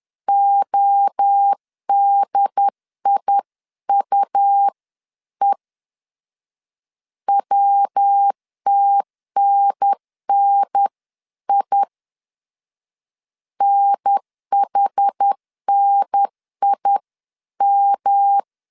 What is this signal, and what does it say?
Morse code, heard as a keyed tone, about 10 wpm, 800 Hz. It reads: ODIUE WTNNI NHNIM
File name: 10wpm